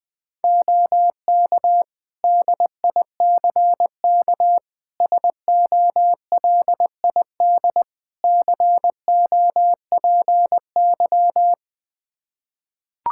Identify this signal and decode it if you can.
Morse code, heard as a keyed tone, about 21 wpm, 700 Hz. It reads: OK DICK SOLID COPY
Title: Practice QSO - Exchange 3 - More Info - 20wpm